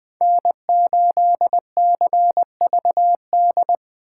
Morse code, heard as a keyed tone, about 20 wpm, 700 Hz. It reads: N8CVD
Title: Speed Racing - USA 1x3 Call Signs 17wpm